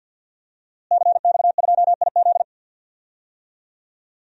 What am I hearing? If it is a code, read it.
Morse code, heard as a keyed tone, about 52 wpm, 700 Hz. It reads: CX2IZ
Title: DXCC-100 Call Signs - 5 Characters - Repeated 2x - 50wpm